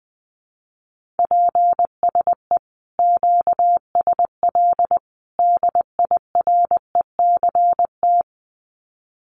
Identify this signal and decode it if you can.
Morse code, heard as a keyed tone, about 20 wpm, 700 Hz. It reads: PSE QSL DIRECT
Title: Common QSO Phrases 20wpm